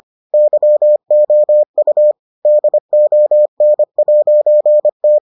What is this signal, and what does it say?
Morse code, heard as a keyed tone, about 25 wpm, 600 Hz. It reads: YOU DON'T